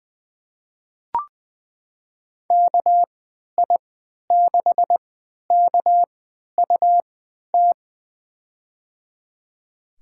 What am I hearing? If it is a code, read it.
Morse code, heard as a keyed tone, about 20 wpm, spaced as for 11 wpm, 700 Hz. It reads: KI6KUT
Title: USA Call Signs - 6 Characters - 3x Character Spacing - 20wpm